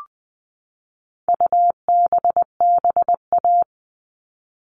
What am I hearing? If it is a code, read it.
Morse code, heard as a keyed tone, about 20 wpm, 700 Hz. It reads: UBBA
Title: All Cut Numbers - 4-Digit 20wpm